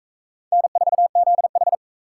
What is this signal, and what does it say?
Morse code, heard as a keyed tone, about 37 wpm, 700 Hz. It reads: N4ZH